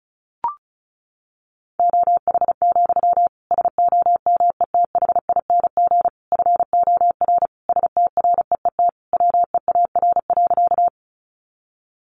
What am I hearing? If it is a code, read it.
Morse code, heard as a keyed tone, about 34 wpm, 700 Hz. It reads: OH, SOMETHING FOR STREET WEAR.
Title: Sentences from Top 1500 Words 35wpm